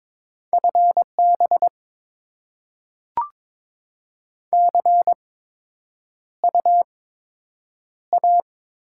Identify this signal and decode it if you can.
Morse code, heard as a keyed tone, about 22 wpm, spaced as for 9 wpm, 700 Hz. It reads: FB CU A
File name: Top 20 QSO Elements - 8x Character Spacing - 22wpm